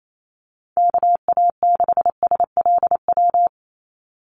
Morse code, heard as a keyed tone, about 28 wpm, 700 Hz. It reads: KA6SLW
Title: USA Call Signs 28wpm